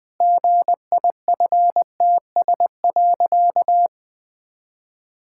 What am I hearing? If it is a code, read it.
Morse code, heard as a keyed tone, about 20 wpm, 700 Hz. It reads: GIFTS.
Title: Pangram Challenge v2 - 20wpm